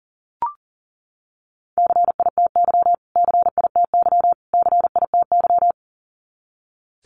Morse code, heard as a keyed tone, about 40 wpm, 700 Hz. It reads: CITY CITY CITY
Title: Top 500 Words Repeated 40wpm